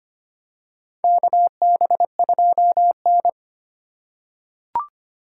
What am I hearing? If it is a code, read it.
Morse code, heard as a keyed tone, about 25 wpm, 700 Hz. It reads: KB2N
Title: USA Call Signs - 3x Character Spacing - 25wpm